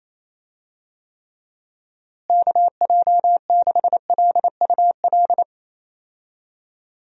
Morse code, heard as a keyed tone, about 27 wpm, 700 Hz. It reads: KJ6LUL
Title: USA Call Signs - 6 Characters - Repeated 2x - 28wpm